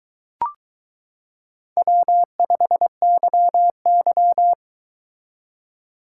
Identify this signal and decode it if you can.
Morse code, heard as a keyed tone, about 22 wpm, 700 Hz. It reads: W5YY